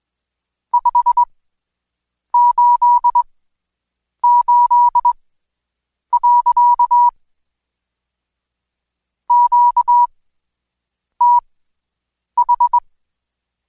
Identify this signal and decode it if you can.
Morse code, heard as a keyed tone, about 20 wpm, spaced as for 7 wpm, 970 Hz. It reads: 588. QTH
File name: Code Quick / AudioDownld2 / Seg112-Day17-DayDrill 17